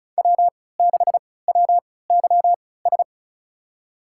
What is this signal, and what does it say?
Morse code, heard as a keyed tone, about 36 wpm, spaced as for 20 wpm, 700 Hz. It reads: W6WYS
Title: USA Call Signs - 3x Character Spacing - 35wpm